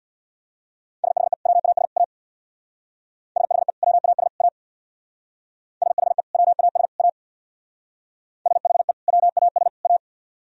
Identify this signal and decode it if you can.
Morse code, heard as a keyed tone, about 57 wpm, 700 Hz. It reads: SHE WAS A SHE WAS A SHE WAS A SHE WAS A
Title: Speed-Racing - Sets of 3 Words 50wpm